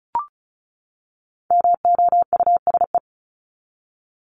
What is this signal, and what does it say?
Morse code, heard as a keyed tone, about 34 wpm, 700 Hz. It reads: MOUSE
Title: Top 501-750 Words 35wpm